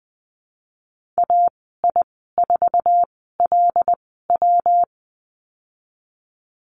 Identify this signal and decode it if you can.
Morse code, heard as a keyed tone, about 20 wpm, spaced as for 15 wpm, 700 Hz. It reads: AI4LW
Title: 140 Lesson 23g - Letter f - Call Signs 20wpm